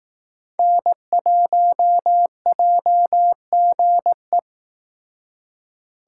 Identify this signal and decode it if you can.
Morse code, heard as a keyed tone, about 19 wpm, 700 Hz. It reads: N1JGE